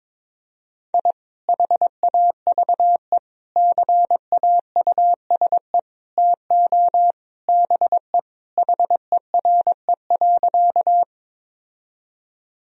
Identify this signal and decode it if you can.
Morse code, heard as a keyed tone, about 22 wpm, 700 Hz. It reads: I HAVE CAUSE TO BE HERE.
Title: Sentences from Top 1000 Words - Review Part 1 - 22wpm